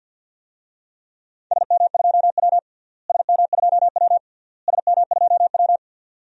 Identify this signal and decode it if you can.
Morse code, heard as a keyed tone, about 52 wpm, 700 Hz. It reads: SM2W SM2W SM2W
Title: DXCC-100 Call Signs - 4 Characters - Repeated 3x - 50wpm